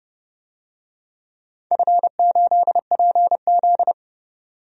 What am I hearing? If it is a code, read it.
Morse code, heard as a keyed tone, about 30 wpm, 700 Hz. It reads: F8PZ
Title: DXCC-100 Call Signs - 4 Characters - 30wpm